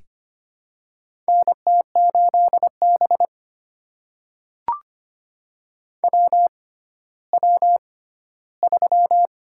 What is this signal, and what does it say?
Morse code, heard as a keyed tone, about 25 wpm, spaced as for 14 wpm, 700 Hz. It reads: NT8B WW3
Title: USA Call Signs - 4 Characters - 6x Character Spacing - 25wpm